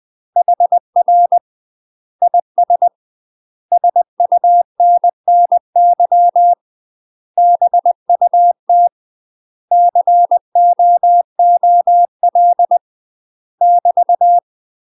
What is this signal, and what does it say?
Morse code, heard as a keyed tone, about 20 wpm, 700 Hz. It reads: HR IS SUNNY BUT COOL <BT>
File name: UK MASTER QSOs TRAINING / 20WPM / UK MASTER QSOs 0040